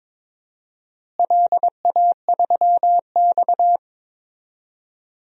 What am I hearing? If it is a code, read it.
Morse code, heard as a keyed tone, about 22 wpm, 700 Hz. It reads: LA3X
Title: DXCC-100 Call Signs - 4 Characters - Repeated 2x - 22wpm